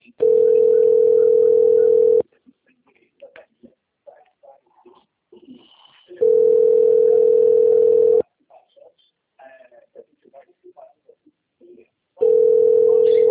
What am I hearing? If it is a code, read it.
Ringback tone (North America), 440 + 480 Hz.